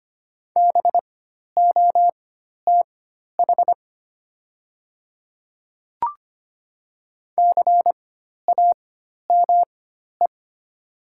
Morse code, heard as a keyed tone, about 25 wpm, spaced as for 12 wpm, 700 Hz. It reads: BOTH CAME
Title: 4x Character Spacing - Top 300 Words - 25wpm